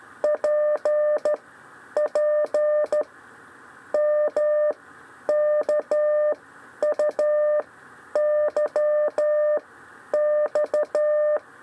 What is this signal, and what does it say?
Morse code, heard as a keyed tone, about 13 wpm, 600 Hz. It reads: P P M K U Y X
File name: 6 lezione cw 1